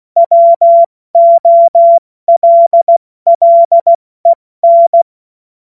Morse code, heard as a keyed tone, about 16 wpm, spaced as for 14 wpm, 680 Hz. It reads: WOLLEN
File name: Morsekurs / Witze / Witz 26. Oktober 2021, 70 F80